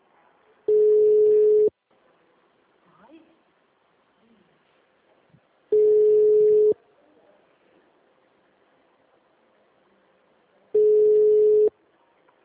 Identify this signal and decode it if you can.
Ringback tone (Europe, 425 Hz), with speech.
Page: Dracena - SP